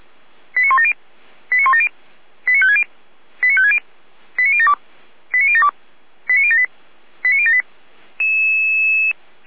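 Five-tone selective call (ZVEI), heard as quick runs of five tones, tones of about 70 ms. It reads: 89170 89170 89560 89560 89072 89072 89078 89078
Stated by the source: ZVEI five-tone call 89150 received by SDR.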